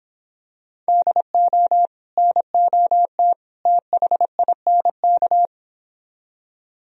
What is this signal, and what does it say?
Morse code, heard as a keyed tone, about 26 wpm, 700 Hz. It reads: DO NOT THINK